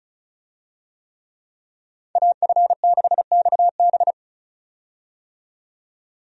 Morse code, heard as a keyed tone, about 36 wpm, 700 Hz. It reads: AF6XB
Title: USA Call Signs - 5 Characters - 35wpm